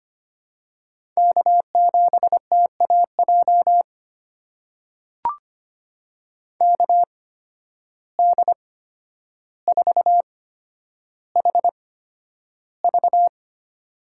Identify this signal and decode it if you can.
Morse code, heard as a keyed tone, about 25 wpm, spaced as for 11 wpm, 700 Hz. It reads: K7TAJ KD4HV
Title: USA Call Signs - 5 Characters - 8x Character Spacing - 25wpm